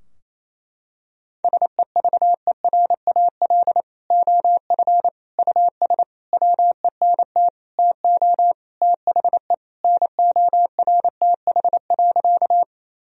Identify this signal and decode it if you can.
Morse code, heard as a keyed tone, about 28 wpm, 700 Hz. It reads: SEVERAL OF US WENT TO THE NORTH.
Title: Sentences from Top 1000 Words - Review Part 2 - 28wpm